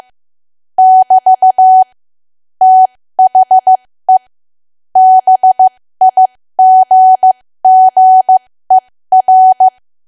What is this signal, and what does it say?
Morse code, heard as a keyed tone, about 15 wpm, spaced as for 13 wpm, 750 Hz. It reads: <BT> THE BIGGER
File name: cwa / W1AW-Level2 / 13WPM